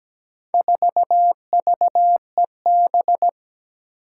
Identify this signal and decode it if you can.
Morse code, heard as a keyed tone, about 17 wpm, 700 Hz. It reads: 4VEB
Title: Four Challenging Letter-Number Combinations 17wpm